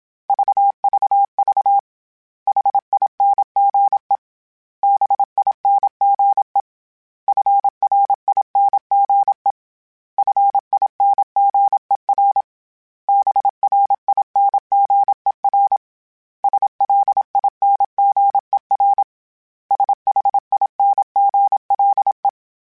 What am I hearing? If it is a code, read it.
Morse code, heard as a keyed tone, about 26 wpm, 800 Hz. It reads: VVV HINGE BINGE FRINGE FINGER BRINGER SLINGER SHINGLE
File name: Special INGs / Special INGs-mp3s / Hidden ings @25